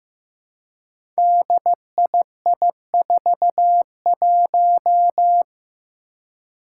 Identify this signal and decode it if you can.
Morse code, heard as a keyed tone, about 15 wpm, 700 Hz. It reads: DII41
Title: Five Challenging Letter-Number Combinations 15wpm